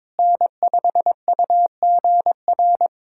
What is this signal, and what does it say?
Morse code, heard as a keyed tone, about 22 wpm, 700 Hz. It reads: N5UGR
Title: USA Callsign DE Callsign - 5 Characters - 22wpm